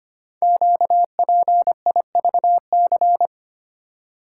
Morse code, heard as a keyed tone, about 25 wpm, 700 Hz. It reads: QPIVC
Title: Code Groups - 5 Letters - 25wpm